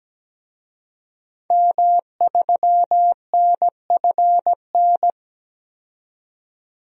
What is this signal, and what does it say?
Morse code, heard as a keyed tone, about 17 wpm, 700 Hz. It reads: M3NFN
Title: DXCC-100 Call Signs - 5 Characters - 17wpm